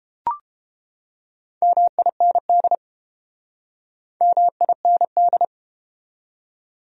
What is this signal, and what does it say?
Morse code, heard as a keyed tone, about 31 wpm, 700 Hz. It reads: MIND MIND